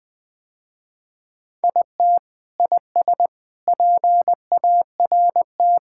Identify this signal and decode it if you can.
Morse code, heard as a keyed tone, about 21 wpm, 700 Hz. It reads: IT IS PART
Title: Sentences with 4-Word Phrases - Spoken Beginning - 20wpm